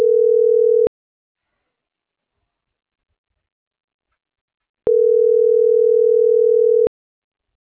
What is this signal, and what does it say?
Ringback tone (North America), 440 + 480 Hz.